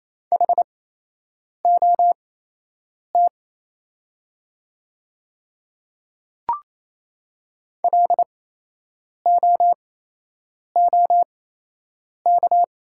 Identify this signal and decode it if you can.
Morse code, heard as a keyed tone, about 27 wpm, spaced as for 8 wpm, 700 Hz. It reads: HOT LOOK
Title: 8x Character Spacing - Top 100 Words - 28wpm